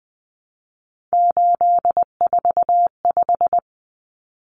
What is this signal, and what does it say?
Morse code, heard as a keyed tone, about 20 wpm, 700 Hz. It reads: 845
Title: Common Cut Numbers - 3-Digit 20wpm